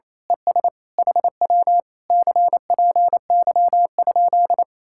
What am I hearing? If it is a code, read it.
Morse code, heard as a keyed tone, about 26 wpm, 700 Hz. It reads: ES HW CPY?
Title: Speed-Racing Common QSO Phrases - 28wpm